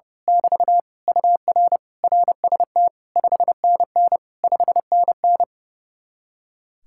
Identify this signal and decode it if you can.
Morse code, heard as a keyed tone, about 30 wpm, 700 Hz. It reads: <BT> UR RST 5NN 5NN
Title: Speed-Racing Common QSO Phrases - 30wpm